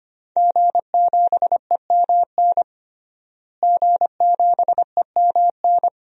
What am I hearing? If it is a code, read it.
Morse code, heard as a keyed tone, about 25 wpm, 700 Hz. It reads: G7EMN G7EMN
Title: DXCC-100 Call Signs - 5 Characters - Repeated 3x - 25wpm